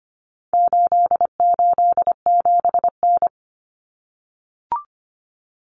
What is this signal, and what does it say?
Morse code, heard as a keyed tone, about 25 wpm, 700 Hz. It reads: 887N
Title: Common Cut Numbers - 4-Digit 25wpm